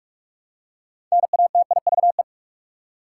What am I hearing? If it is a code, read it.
Morse code, heard as a keyed tone, about 46 wpm, 700 Hz. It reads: NATIVE
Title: Six Letter Words - 45wpm